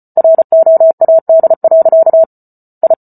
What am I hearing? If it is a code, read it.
Morse code, heard as a keyed tone, about 34 wpm, 650 Hz. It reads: ROAD. I